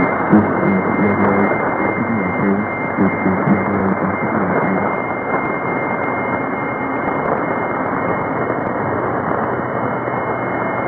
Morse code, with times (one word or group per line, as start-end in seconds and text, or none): none